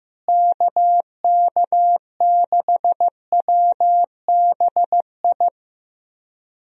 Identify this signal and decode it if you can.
Morse code, heard as a keyed tone, about 15 wpm, 700 Hz. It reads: KK6WBI
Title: USA Call Signs - 6 Characters - Repeated 3x - 15wpm